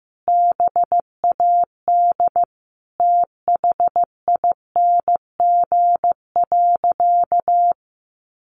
Morse code, heard as a keyed tone, about 15 wpm, 700 Hz. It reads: BAD THING.
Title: Sentences from Top 1200 Words 15wpm